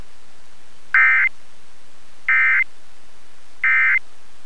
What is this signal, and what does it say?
Emergency Alert System (SAME) tones, heard as harsh buzzing bursts. It is heard from 0.8 to 4.1 s.